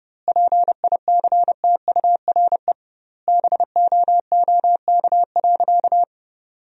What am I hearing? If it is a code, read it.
Morse code, heard as a keyed tone, about 32 wpm, 700 Hz. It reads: PICTURE BOOK.
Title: 2x Word-Spacing - Sentences from Top 500 Words - 30wpm